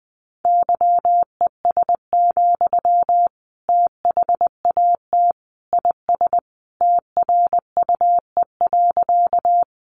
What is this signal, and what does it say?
Morse code, heard as a keyed tone, about 20 wpm, 700 Hz. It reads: YES, THAT IS TRUE.
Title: Sentences from Top 500 Words 20wpm